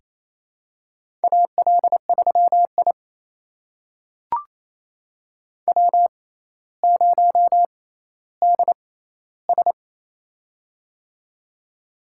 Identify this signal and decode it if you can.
Morse code, heard as a keyed tone, about 27 wpm, spaced as for 14 wpm, 700 Hz. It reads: AL3I W0DS
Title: USA Call Signs - 6x Character Spacing - 28wpm